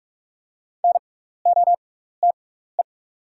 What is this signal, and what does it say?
Morse code, heard as a keyed tone, about 44 wpm, spaced as for 15 wpm, 700 Hz. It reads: NOTE
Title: 6x Character Spacing - Top 500 Words - 45wpm